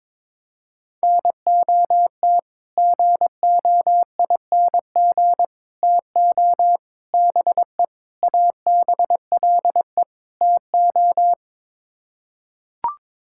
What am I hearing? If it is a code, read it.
Morse code, heard as a keyed tone, about 22 wpm, 700 Hz. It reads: NOT GOING TO BE ABLE TO
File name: practice / 22wpm / Sets of 6 Words - 22wpm